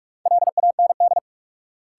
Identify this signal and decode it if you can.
Morse code, heard as a keyed tone, about 46 wpm, 700 Hz. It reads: LAND
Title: Sentences from Top 500 Words - New Word Drill - 45wpm